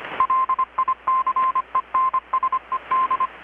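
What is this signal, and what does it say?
Morse code, heard as a keyed tone, about 25 wpm, 1040 Hz, with radio noise. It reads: LICENSED
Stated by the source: Morse from KPH's Night of Nights XX maritime radiotelegraphy event, received in Chiapas, Mexico about 3,800 km away on a Kenwood R-600 with a grounded 10 m long-wire antenna.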